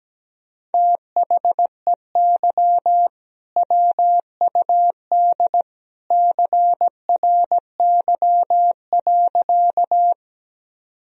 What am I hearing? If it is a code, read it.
Morse code, heard as a keyed tone, about 17 wpm, 700 Hz. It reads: THEY WUD CRY.
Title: Sentences from Top 500 Words and Abbreviations 17wpm